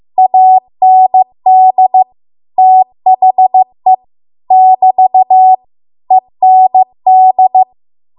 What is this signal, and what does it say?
Morse code, heard as a keyed tone, about 15 wpm, 750 Hz. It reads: AND THE <BT> END